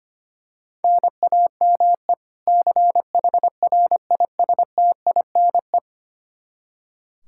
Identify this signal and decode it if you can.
Morse code, heard as a keyed tone, about 25 wpm, 700 Hz. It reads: NAME CHRISTINE
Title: Practice QSO - Exchange 1 - Beginning - 25wpm